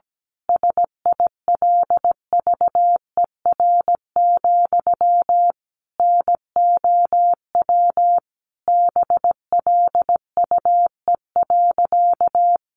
Morse code, heard as a keyed tone, about 17 wpm, 700 Hz. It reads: SILVER, NOW BLUE.
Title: A Sound Of Thunder 17wpm